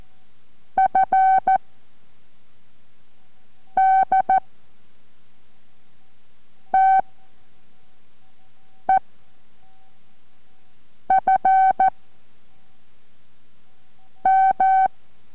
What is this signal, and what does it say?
Morse code, heard as a keyed tone, about 14 wpm, 760 Hz. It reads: F D T E F M